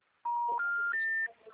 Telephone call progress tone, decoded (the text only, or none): special information tone